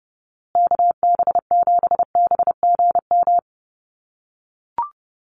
Morse code, heard as a keyed tone, about 28 wpm, 700 Hz. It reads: KB7BGM
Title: USA Call Signs Repeated 3x 30wpm